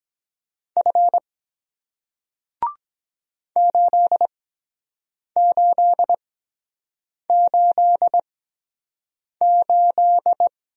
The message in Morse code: F 8888

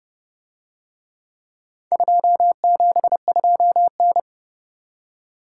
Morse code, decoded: 272N

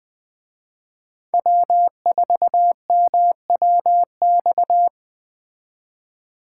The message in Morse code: W4MWX

